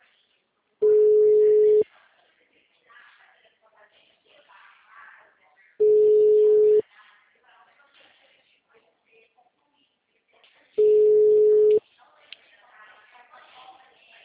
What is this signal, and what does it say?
Ringback tone (Europe, 425 Hz), with speech.